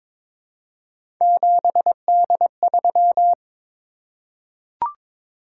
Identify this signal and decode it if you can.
Morse code, heard as a keyed tone, about 22 wpm, 700 Hz. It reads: 7D3
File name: practice / 22wpm / Three Letter-Number Combinations - 22wpm